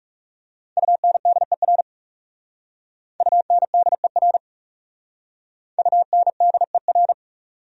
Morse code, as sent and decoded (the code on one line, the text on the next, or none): ..- -. -.. .-. / ..- -. -.. .-. / ..- -. -.. .-.
UNDER UNDER UNDER